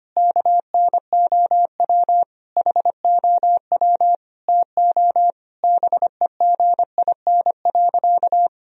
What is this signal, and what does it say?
Morse code, heard as a keyed tone, about 25 wpm, 700 Hz. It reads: KNOW HOW TO BEGIN.
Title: Sentences from Top 1000 Words - Review Part 1 - 25wpm